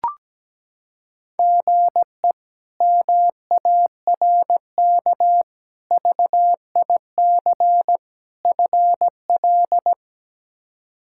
Morse code, heard as a keyed tone, about 17 wpm, 700 Hz. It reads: GE MARK VIC FL
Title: K1USN SST QSO Practice - 17wpm